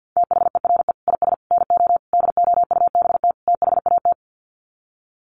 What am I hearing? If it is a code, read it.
Morse code, heard as a keyed tone, about 49 wpm, 700 Hz. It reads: THERE IS NO DOUBT THAT
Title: Sets of 5 Words 50wpm